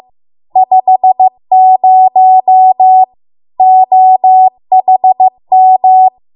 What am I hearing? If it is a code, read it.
Morse code, heard as a keyed tone, about 15 wpm, 750 Hz. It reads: 50 OHM